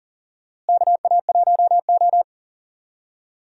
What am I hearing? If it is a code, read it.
Morse code, heard as a keyed tone, about 40 wpm, 700 Hz. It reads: KA1O
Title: USA Call Signs - 4 Characters - Repeated 3x - 40wpm